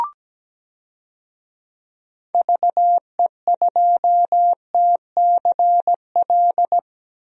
Morse code, heard as a keyed tone, about 17 wpm, 700 Hz. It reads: VE2TCL
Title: DXCC-100 Call Signs - 6 Characters - 17wpm